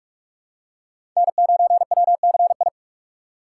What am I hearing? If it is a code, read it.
Morse code, heard as a keyed tone, about 44 wpm, 700 Hz. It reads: N9WCI
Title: USA Call Signs - 5 Characters - 6x Character Spacing - 45wpm